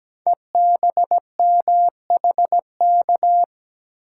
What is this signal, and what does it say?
Morse code, heard as a keyed tone, about 17 wpm, 700 Hz. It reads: EBMHK
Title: Code Groups - 5 Letters - 17wpm